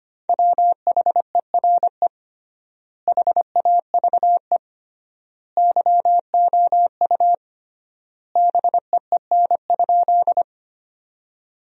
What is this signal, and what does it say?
Morse code, heard as a keyed tone, about 25 wpm, 700 Hz. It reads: WHERE HAVE YOU BEEN?
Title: 3x Word-Spacing - Sentences from Top 200 Words - 25wpm